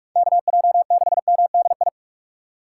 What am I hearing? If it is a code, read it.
Morse code, heard as a keyed tone, about 46 wpm, 700 Hz. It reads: KJ6MDI
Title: USA Call Signs - 6 Characters - 2x Character Spacing - 45wpm